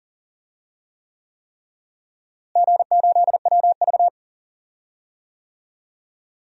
Morse code, heard as a keyed tone, about 40 wpm, 700 Hz. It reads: G8WV
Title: DXCC-100 Call Signs - 4 Characters - Repeated 3x - 40wpm